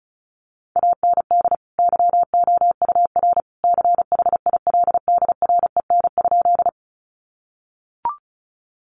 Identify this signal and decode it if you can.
Morse code, heard as a keyed tone, about 34 wpm, 700 Hz. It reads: AND YOUR CHILDREN?
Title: Sentences from Top 300 Words 35wpm